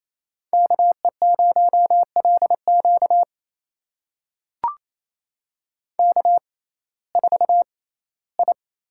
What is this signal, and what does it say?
Morse code, heard as a keyed tone, about 28 wpm, spaced as for 17 wpm, 700 Hz. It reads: KE0LQ K4I
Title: USA Call Signs - 5 Characters - 6x Character Spacing - 28wpm